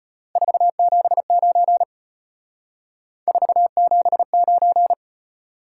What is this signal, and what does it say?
Morse code, heard as a keyed tone, about 32 wpm, 700 Hz. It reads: 479 479